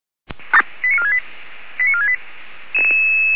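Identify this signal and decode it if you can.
Five-tone selective call (ZVEI), heard as quick runs of five tones, tones of about 70 ms. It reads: 89468 89468
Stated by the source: ZVEI five-tone selective call received by SDR.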